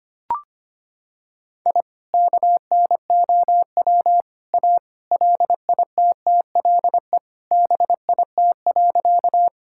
Morse code, heard as a keyed tone, about 25 wpm, 700 Hz. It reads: I KNOW A LITTLE BIT.
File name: practice / 25wpm / Sentences with 3-Word Phrases - 25wpm